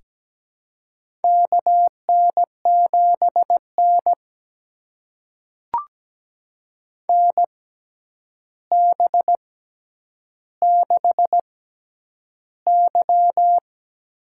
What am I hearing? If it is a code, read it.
Morse code, heard as a keyed tone, about 17 wpm, spaced as for 9 wpm, 700 Hz. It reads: KN7N NB6Y
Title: USA Call Signs - 6x Character Spacing - 17wpm